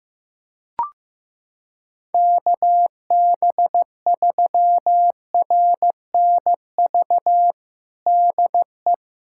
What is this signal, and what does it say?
Morse code, heard as a keyed tone, about 15 wpm, 700 Hz. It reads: KB3RNV DE
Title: USA Callsign DE Callsign 15wpm